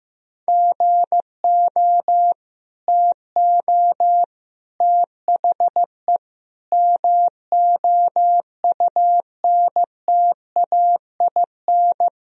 Morse code, heard as a keyed tone, about 15 wpm, 700 Hz. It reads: GO TO THE MOUNTAIN
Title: Sentences from Top 1000 Words - Review Part 1 - 15wpm